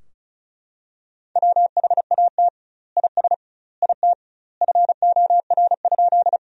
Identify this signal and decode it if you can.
Morse code, heard as a keyed tone, about 34 wpm, 700 Hz. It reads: WHAT IS IT FOR?
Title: 2x Word-Spacing - Sentences from Top 100 Words - 35wpm